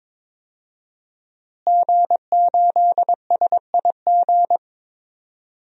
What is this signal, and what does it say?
Morse code, heard as a keyed tone, about 22 wpm, 700 Hz. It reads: G8SIG
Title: DXCC-100 Call Signs - 5 Characters - Repeated 3x - 22wpm